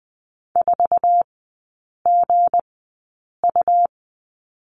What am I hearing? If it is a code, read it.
Morse code, heard as a keyed tone, about 20 wpm, spaced as for 8 wpm, 700 Hz. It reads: 4GU
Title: 178 Lesson 29e - Slash - Three Characters 20wpm